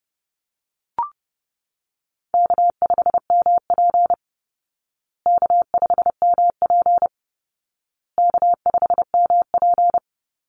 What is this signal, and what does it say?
Morse code, heard as a keyed tone, about 29 wpm, 700 Hz. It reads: K5MP K5MP K5MP
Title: Call Signs Repeated 2x - 30wpm